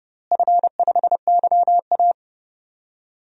Morse code, heard as a keyed tone, about 30 wpm, 700 Hz. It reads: F5YA
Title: DXCC-100 Call Signs - 4 Characters - Repeated 3x - 30wpm